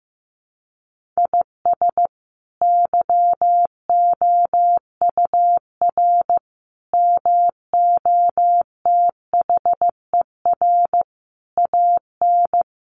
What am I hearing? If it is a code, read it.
Morse code, heard as a keyed tone, about 15 wpm, 700 Hz. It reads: IS YOUR MOTHER AN
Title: Sentences from Top 1500 Words 15wpm